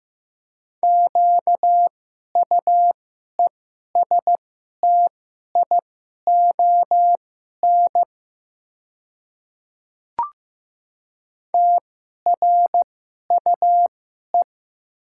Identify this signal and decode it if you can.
Morse code, heard as a keyed tone, about 15 wpm, spaced as for 11 wpm, 700 Hz. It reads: QUESTION TRUE
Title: 2x Character Spacing - Top 500 Words - 15wpm